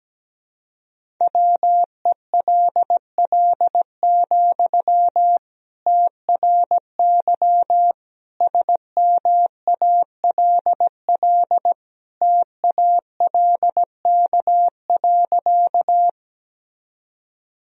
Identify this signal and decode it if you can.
Morse code, heard as a keyed tone, about 17 wpm, 700 Hz. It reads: WELL, TRY SMALL TALK.
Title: Sentences from Top 1000 Words - Review Part 1 - 17wpm